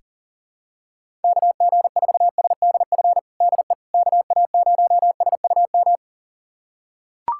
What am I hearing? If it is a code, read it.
Morse code, heard as a keyed tone, about 40 wpm, 700 Hz. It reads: KG4SDF DE KA0SUM
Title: USA Callsign DE Callsign - 6 Characters - 40wpm